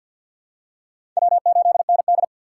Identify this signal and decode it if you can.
Morse code, heard as a keyed tone, about 46 wpm, 690 Hz. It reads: W8ND
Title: USA Call Signs - 4 Characters - 6x Character Spacing - 50wpm